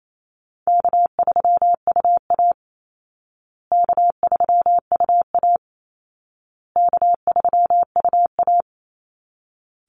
Morse code, heard as a keyed tone, about 28 wpm, 700 Hz. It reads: K3UA K3UA K3UA
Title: Easy Call Signs Repeated 3x - 28wpm